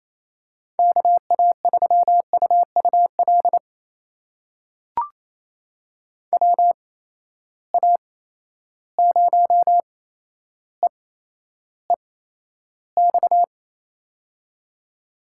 Morse code, heard as a keyed tone, about 28 wpm, spaced as for 12 wpm, 700 Hz. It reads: KA3UUL WA0EEX
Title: USA Call Signs - 6 Characters - 8x Character Spacing - 28wpm